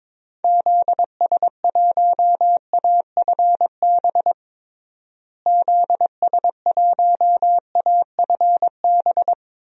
Morse code, heard as a keyed tone, about 22 wpm, 700 Hz. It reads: ZS1AFB ZS1AFB
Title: DXCC-100 Call Signs - 6 Characters - Repeated 3x - 22wpm